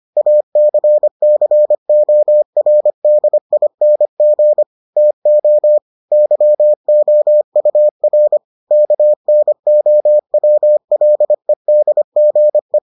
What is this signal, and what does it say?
Morse code, heard as a keyed tone, about 25 wpm, 600 Hz. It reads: ACCORDING TO YOUR KNOWLEDGE